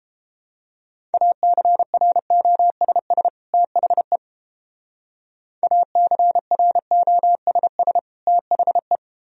Morse code, heard as a keyed tone, about 29 wpm, 700 Hz. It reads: ACROSS THE ACROSS THE